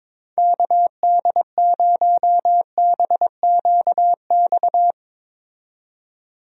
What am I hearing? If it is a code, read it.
Morse code, heard as a keyed tone, about 21 wpm, 700 Hz. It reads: KD0BQX
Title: USA Call Signs - 6 Characters - 22wpm